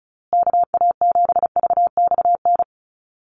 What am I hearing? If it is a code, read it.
Morse code, heard as a keyed tone, about 34 wpm, 700 Hz. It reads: KA7VXN
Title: USA Call Signs Repeated 3x 35wpm